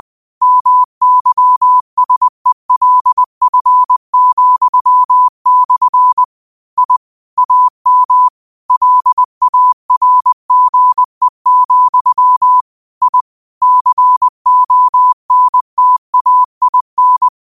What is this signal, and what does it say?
Morse code, heard as a keyed tone, about 20 wpm, 1000 Hz. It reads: MYSELF,/ I AM LARGE, I CONTAIN